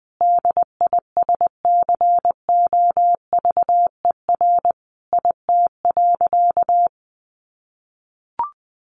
Morse code, heard as a keyed tone, about 20 wpm, 700 Hz. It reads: DISCOVER IT.
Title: Sentences from Top 1100 Words 20wpm